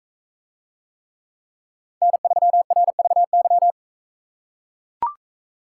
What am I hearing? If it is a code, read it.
Morse code, heard as a keyed tone, about 40 wpm, 700 Hz. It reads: N3RVY